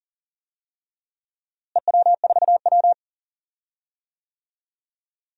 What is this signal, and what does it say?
Morse code, heard as a keyed tone, about 40 wpm, 700 Hz. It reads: EW4W
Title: DXCC-100 Call Signs - 4 Characters - Repeated 3x - 40wpm